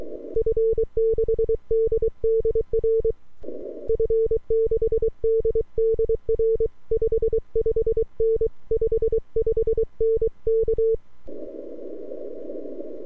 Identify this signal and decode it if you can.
Morse code, heard as a keyed tone, about 23 wpm, 450 Hz. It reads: F6DDR F6DDR 55N 55N K
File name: n6tr / f6ddr